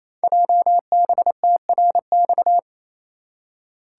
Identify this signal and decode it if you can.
Morse code, heard as a keyed tone, about 27 wpm, 700 Hz. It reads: JBTRX
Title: Code Groups - 5 Letters - 28wpm